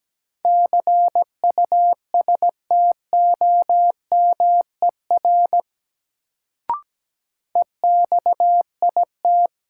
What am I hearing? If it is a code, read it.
Morse code, heard as a keyed tone, about 17 wpm, 700 Hz. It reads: CUSTOMER EXIT
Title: Sentences from Top 1100 Words - New Word Drill Alt - 17wpm